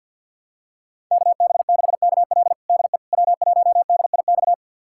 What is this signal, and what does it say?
Morse code, heard as a keyed tone, about 51 wpm, 700 Hz. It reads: KB6KL DE W1DIX
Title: USA Callsign DE Callsign 50wpm